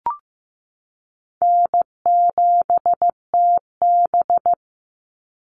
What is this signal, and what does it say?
Morse code, heard as a keyed tone, about 15 wpm, 700 Hz. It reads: N7TB
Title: Easy Call Signs Repeated 3x - 15wpm